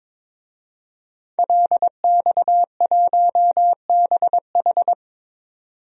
Morse code, heard as a keyed tone, about 21 wpm, 700 Hz. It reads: LX1BH
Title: DXCC-100 Call Signs - 5 Characters - 22wpm